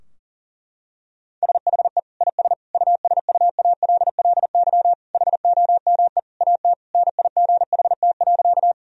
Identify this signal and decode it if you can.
Morse code, heard as a keyed tone, about 37 wpm, 700 Hz. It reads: SHE IS USUALLY HOME AT NIGHT.
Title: Sentences from Top 1000 Words - Review Part 3 - 40wpm